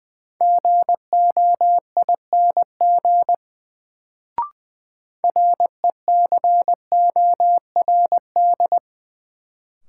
Morse code, heard as a keyed tone, about 20 wpm, 700 Hz. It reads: GOING RECORD
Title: Sentences from Top 500 Words - New Word Drill - 20wpm